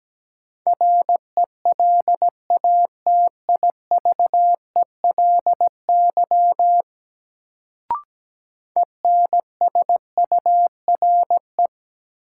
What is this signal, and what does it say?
Morse code, heard as a keyed tone, about 17 wpm, 700 Hz. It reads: RELATIVELY ENSURE
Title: Sentences from Top 1100 Words - New Word Drill - 17wpm